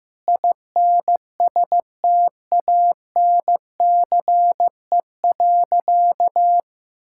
Morse code, heard as a keyed tone, about 16 wpm, 700 Hz. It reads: INSTANCE.
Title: Sentences from Top 1000 Words - Review Part 5 - 15wpm